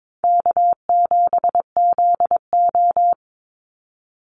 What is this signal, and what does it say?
Morse code, heard as a keyed tone, about 22 wpm, 700 Hz. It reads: K7ZO
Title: NA SOTA Chaser Callsigns 22wpm